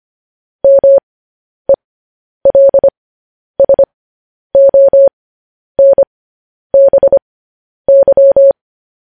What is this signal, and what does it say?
Morse code, heard as a keyed tone, about 25 wpm, spaced as for 10 wpm, 550 Hz. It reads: MELSONBY
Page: news-headlines-in-morse-code-at-10-wpm